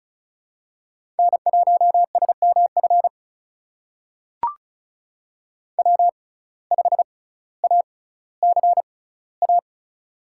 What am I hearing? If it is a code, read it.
Morse code, heard as a keyed tone, about 34 wpm, spaced as for 18 wpm, 700 Hz. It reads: N1SMF W5ACA
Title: USA Call Signs - 5 Characters - 6x Character Spacing - 35wpm